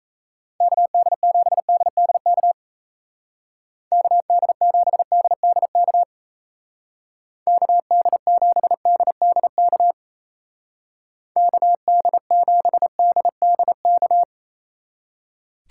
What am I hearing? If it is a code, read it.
Morse code, heard as a keyed tone, about 34 wpm, 700 Hz. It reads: KD7DDK KD7DDK KD7DDK KD7DDK
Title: Speed Racing - USA 2x3 Call Signs 28wpm